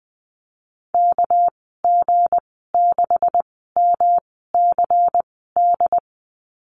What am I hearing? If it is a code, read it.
Morse code, heard as a keyed tone, about 20 wpm, spaced as for 14 wpm, 700 Hz. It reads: KG6MCD
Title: 245 Lesson 39g - Letter z - Call Signs 20wpm